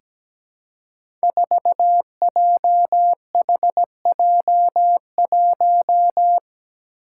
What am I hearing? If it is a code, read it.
Morse code, heard as a keyed tone, about 17 wpm, 700 Hz. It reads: 4JHJ1
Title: Five Challenging Letter-Number Combinations 17wpm